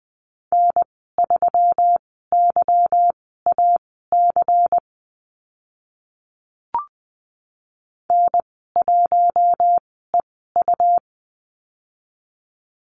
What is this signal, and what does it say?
Morse code, heard as a keyed tone, about 20 wpm, spaced as for 15 wpm, 700 Hz. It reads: N3YAC N1EU
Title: 167 Lesson 27g - Number 7 - Call Signs 20wpm